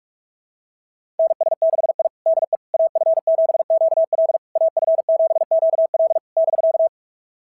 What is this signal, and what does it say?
Morse code, heard as a keyed tone, about 42 wpm, 690 Hz. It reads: NS6I DE AF7QL AF7QL <BK>
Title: SOTA QSO Practice 45wpm